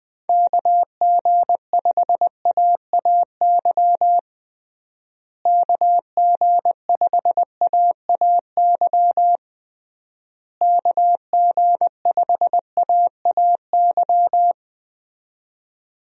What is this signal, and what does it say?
Morse code, heard as a keyed tone, about 19 wpm, 700 Hz. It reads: KG5AAY KG5AAY KG5AAY
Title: USA Call Signs - 6 Characters - Repeated 3x - 20wpm